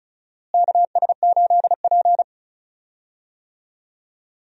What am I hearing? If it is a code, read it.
Morse code, heard as a keyed tone, about 36 wpm, 700 Hz. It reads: KS8P
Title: USA Call Signs - 4 Characters - Repeated 3x - 35wpm